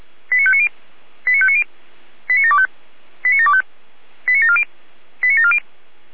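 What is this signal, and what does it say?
Five-tone selective call (ZVEI), heard as quick runs of five tones, tones of about 70 ms. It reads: 89590 89590 89725 89725 89740 89740
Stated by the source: Five-tone ZVEI selective call received on an SDR.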